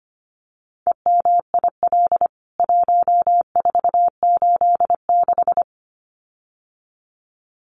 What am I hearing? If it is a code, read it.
Morse code, heard as a keyed tone, about 25 wpm, 690 Hz. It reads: EMIL 1486
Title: CWT Exchange 25wpm